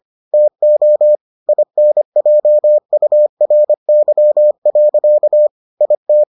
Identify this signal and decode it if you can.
Morse code, heard as a keyed tone, about 25 wpm, 600 Hz. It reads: TO INJURY. IT